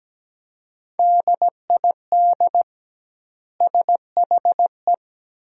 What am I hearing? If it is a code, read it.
Morse code, heard as a keyed tone, about 17 wpm, 700 Hz. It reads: DID SHE
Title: 2x Word-Spacing - Sentences from Top 500 Words - 17wpm